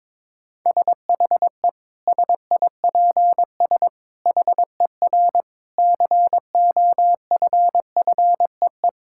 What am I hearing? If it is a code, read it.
Morse code, heard as a keyed tone, about 22 wpm, 700 Hz. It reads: SHE SIPS HER COFFEE
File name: Difficult Sentences with Hints - 22wpm